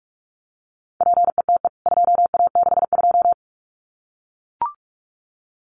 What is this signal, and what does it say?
Morse code, heard as a keyed tone, about 44 wpm, 700 Hz. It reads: PETE 2A62
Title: CWT Exchange 45wpm